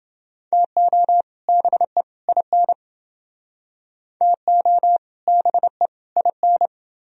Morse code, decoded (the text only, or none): TO BE IN TO BE IN